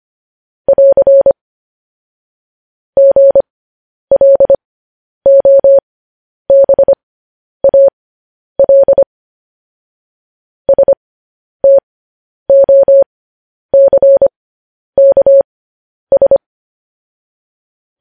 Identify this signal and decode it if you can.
Morse code, heard as a keyed tone, about 25 wpm, spaced as for 10 wpm, 550 Hz. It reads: <AR> GLOBAL STOCKS
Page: News Headlines in Morse Code at 10 WPM